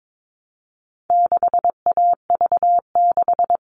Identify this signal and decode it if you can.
Morse code, heard as a keyed tone, about 22 wpm, 700 Hz. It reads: 6AV6
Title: All Cut Numbers - 4-Digit 22wpm